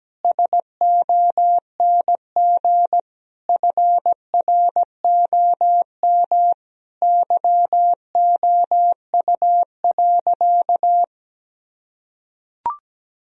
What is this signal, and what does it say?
Morse code, heard as a keyed tone, about 17 wpm, 700 Hz. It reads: SONG FROM YOU.
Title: Sentences from Top 1000 Words - Review Part 2 - 17wpm